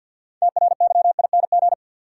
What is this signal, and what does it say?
Morse code, heard as a keyed tone, about 47 wpm, 700 Hz. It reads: TRYING
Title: Sets of 2 Words - New Word Drill - 50wpm